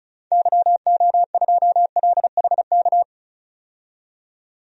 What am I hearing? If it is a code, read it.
Morse code, heard as a keyed tone, about 36 wpm, 700 Hz. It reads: YO2LHK
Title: DXCC-100 Call Signs - 6 Characters - 35wpm